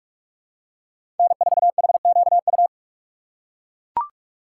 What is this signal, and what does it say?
Morse code, heard as a keyed tone, about 41 wpm, 700 Hz. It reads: N4HQU
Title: USA Call Signs - 5 Characters - 6x Character Spacing - 45wpm